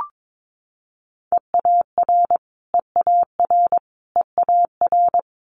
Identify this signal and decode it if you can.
Morse code, heard as a keyed tone, about 22 wpm, 700 Hz. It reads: EAR EAR EAR
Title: Multi-letter Phonemes Repeated 3x 22wpm